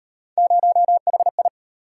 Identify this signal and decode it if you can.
Morse code, heard as a keyed tone, about 37 wpm, 700 Hz. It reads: 0HI